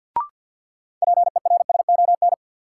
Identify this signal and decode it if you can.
Morse code, heard as a keyed tone, about 52 wpm, 700 Hz. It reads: PERSON
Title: Sentences from Top 400 Words - New Word Drill - 50wpm